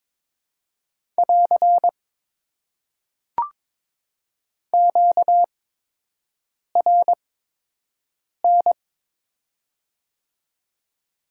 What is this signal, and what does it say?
Morse code, heard as a keyed tone, about 23 wpm, spaced as for 6 wpm, 700 Hz. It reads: <AR> QRN
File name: Top 90 QSO Elements - 8x Character Spacing - 22wpm